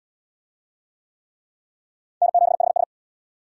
Morse code, heard as a keyed tone, about 66 wpm, 700 Hz. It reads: N7HS